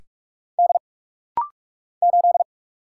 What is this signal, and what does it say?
Morse code, heard as a keyed tone, about 44 wpm, 700 Hz. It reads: D 8
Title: Single Letter-Number Alt - Rapid-Fire 45wpm